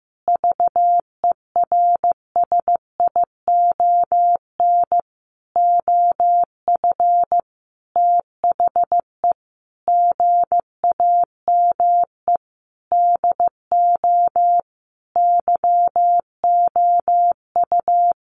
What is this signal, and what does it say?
Morse code, heard as a keyed tone, about 15 wpm, 700 Hz. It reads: VERSION OF THE GAME DO YOU
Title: Sentences from Top 1400 Words 15wpm